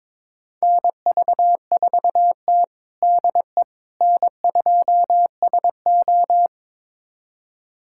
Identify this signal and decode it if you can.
Morse code, heard as a keyed tone, about 21 wpm, 700 Hz. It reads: NV4T DE N2SO
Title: USA Callsign DE Callsign - 4 Characters - 22wpm